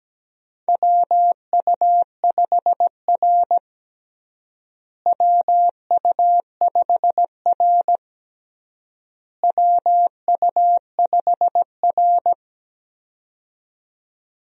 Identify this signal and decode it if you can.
Morse code, heard as a keyed tone, about 17 wpm, 700 Hz. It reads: WU5R WU5R WU5R
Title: USA Call Signs - 4 Characters - Repeated 3x - 17wpm